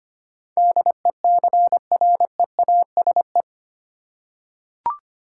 Morse code, heard as a keyed tone, about 25 wpm, 700 Hz. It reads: DECREASE
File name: 8 Letter Words - First 3 Letters Given - 25wpm